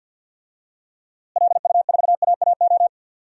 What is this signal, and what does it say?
Morse code, heard as a keyed tone, about 48 wpm, 700 Hz. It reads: LU4AAO
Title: DXCC-100 Call Signs - 6 Characters - Repeated 3x - 50wpm